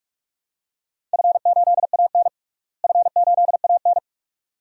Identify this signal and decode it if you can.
Morse code, heard as a keyed tone, about 46 wpm, 700 Hz. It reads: F8AN F8AN
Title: DXCC-100 Call Signs - 4 Characters - Repeated 2x - 45wpm